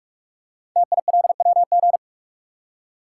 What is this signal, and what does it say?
Morse code, heard as a keyed tone, about 44 wpm, 700 Hz. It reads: TILWG
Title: Code Groups - 5 Letters - 45wpm